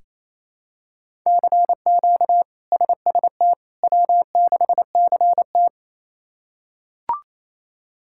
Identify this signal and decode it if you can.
Morse code, heard as a keyed tone, about 28 wpm, 700 Hz. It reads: CQ SST W6CT
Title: K1USN SST QSO Practice - 28wpm